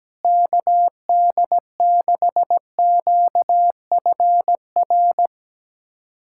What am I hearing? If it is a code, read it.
Morse code, heard as a keyed tone, about 17 wpm, 700 Hz. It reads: KD6QFR
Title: USA Callsign DE Callsign 17wpm